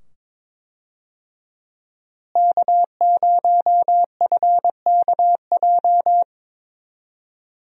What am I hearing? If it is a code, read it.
Morse code, heard as a keyed tone, about 22 wpm, 700 Hz. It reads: K0FKJ
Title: USA Call Signs - 5 Characters - Repeated 2x - 22wpm